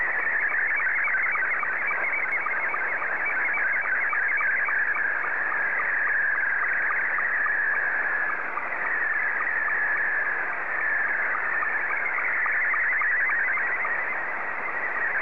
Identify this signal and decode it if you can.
SSTV (slow-scan television), Scottie 2.
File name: sstv / S2-JR1ENT